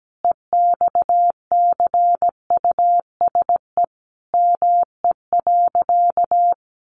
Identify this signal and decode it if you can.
Morse code, heard as a keyed tone, about 17 wpm, 700 Hz. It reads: EXCUSE ME.
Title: Sentences from Top 700 Words 17wpm